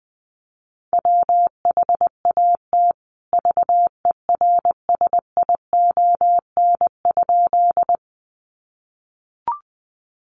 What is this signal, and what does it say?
Morse code, heard as a keyed tone, about 20 wpm, 700 Hz. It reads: WHAT VERSION?
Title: Sentences from Top 1400 Words 20wpm